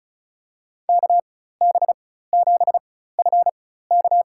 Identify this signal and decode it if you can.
Morse code, heard as a keyed tone, about 36 wpm, spaced as for 16 wpm, 700 Hz. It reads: KB7FK
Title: USA Call Signs - 5 Characters - 4x Character Spacing - 35wpm